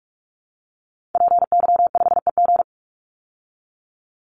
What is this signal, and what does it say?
Morse code, heard as a keyed tone, about 44 wpm, 700 Hz. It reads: PY5EG
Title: Call Signs Repeated 2x - 45wpm